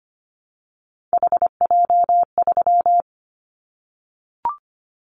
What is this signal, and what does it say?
Morse code, heard as a keyed tone, about 25 wpm, 700 Hz. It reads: HJ3
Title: Callsign Prefixes for Top 100 DXCC - 25wpm